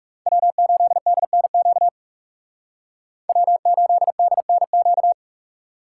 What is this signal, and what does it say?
Morse code, heard as a keyed tone, about 36 wpm, 690 Hz. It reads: W8DNQ W8DNQ
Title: Speed Racing - USA Call Signs 30wpm